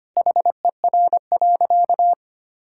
Morse code, heard as a keyed tone, about 25 wpm, 700 Hz. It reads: HER.
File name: practice / 25wpm / Sentences with 2-Word Phrases - 25wpm